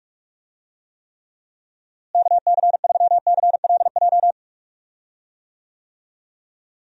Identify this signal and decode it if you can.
Morse code, heard as a keyed tone, about 41 wpm, 700 Hz. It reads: KC3CLJ
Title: USA Call Signs - 6 Characters - Repeated 3x - 45wpm